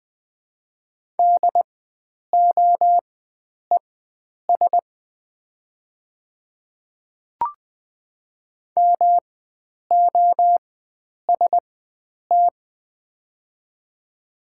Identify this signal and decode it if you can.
Morse code, heard as a keyed tone, about 20 wpm, spaced as for 9 wpm, 700 Hz. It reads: DOES MOST
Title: 4x Character Spacing - Top 300 Words - 20wpm